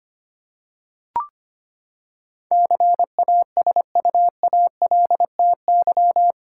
Morse code, heard as a keyed tone, about 25 wpm, 700 Hz. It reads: CASUALTY
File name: Eight Letter Words - 25wpm